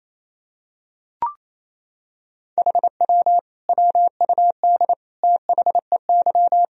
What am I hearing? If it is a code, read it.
Morse code, heard as a keyed tone, about 28 wpm, 700 Hz. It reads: HW WUD THEY
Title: Sentences from Top 300 Words and Abbreviations 28wpm